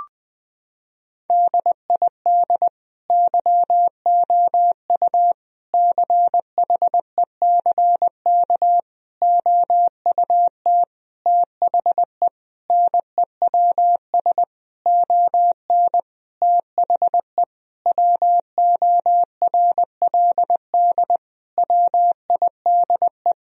Morse code, heard as a keyed tone, about 20 wpm, 700 Hz. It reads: DID YOU CHECK OUT THE NEWS ON THE WORLD WIDE
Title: Sentences from Top 1000 Words - Review Part 4 - 20wpm